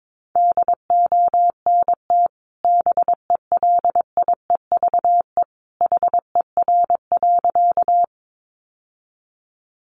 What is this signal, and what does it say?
Morse code, heard as a keyed tone, about 22 wpm, 700 Hz. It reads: DONT BELIEVE HER.
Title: Sentences from Top 1000 Words 22wpm